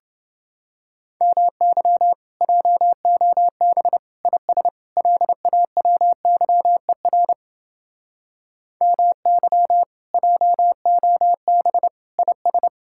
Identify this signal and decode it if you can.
Morse code, heard as a keyed tone, about 31 wpm, 700 Hz. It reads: MY JOB IS LAWYER MY JOB IS